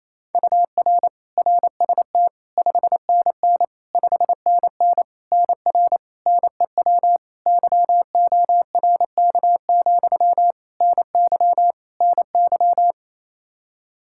Morse code, heard as a keyed tone, about 28 wpm, 700 Hz. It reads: UR RST 5NN 5NN NR NEW YORK, NY NY
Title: K1USN SST QSO Practice 28wpm